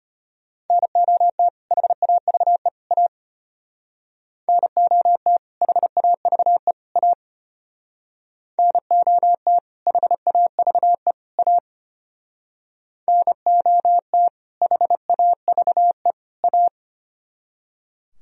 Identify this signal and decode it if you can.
Morse code, heard as a keyed tone, about 32 wpm, 700 Hz. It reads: NOT HAVE A NOT HAVE A NOT HAVE A NOT HAVE A